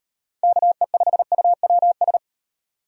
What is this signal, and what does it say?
Morse code, heard as a keyed tone, about 38 wpm, 700 Hz. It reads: KE5UWS